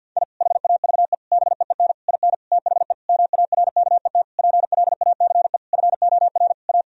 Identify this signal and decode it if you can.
Morse code, heard as a keyed tone, about 52 wpm, 700 Hz. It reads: I HAVE BEEN IN THE MARKET PLACE FOR A
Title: Sentences from Top 1000 Words - Review Part 3 - 50wpm